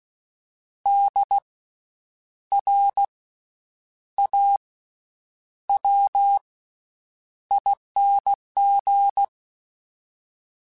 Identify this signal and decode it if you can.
Morse code, heard as a keyed tone, about 16 wpm, spaced as for 8 wpm, 800 Hz. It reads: DRAWING
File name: Special INGs / Special INGs-mp3s / VerySlowToFaster Ing @15